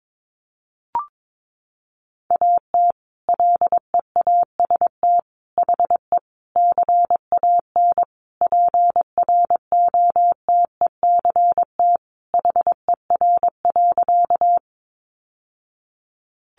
Morse code, heard as a keyed tone, about 22 wpm, 700 Hz. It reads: AT LEAST HE CAN PROTECT HER.
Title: Sentences from Top 900 Words 22wpm